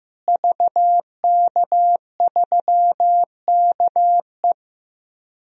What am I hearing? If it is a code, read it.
Morse code, heard as a keyed tone, about 15 wpm, 700 Hz. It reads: VK3KE
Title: DXCC-100 Call Signs - 5 Characters - Repeated 3x - 15wpm